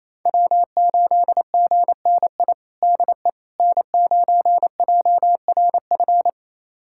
Morse code, heard as a keyed tone, about 28 wpm, 700 Hz. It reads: W8GNI DE N9JRF
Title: USA Callsign DE Callsign - 5 Characters - 28wpm